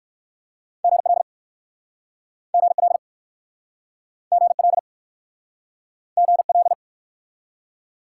Morse code, heard as a keyed tone, about 61 wpm, 700 Hz. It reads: GL GL GL GL